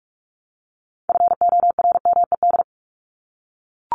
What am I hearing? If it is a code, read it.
Morse code, heard as a keyed tone, about 44 wpm, 700 Hz. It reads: FORMED
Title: ED Suffix - Top 100 Words with Base Words 45wpm